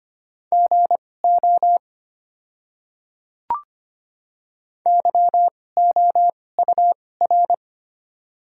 Morse code, heard as a keyed tone, about 25 wpm, spaced as for 18 wpm, 700 Hz. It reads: GO YOUR
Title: 2x Character Spacing - Top 100 Words - 25wpm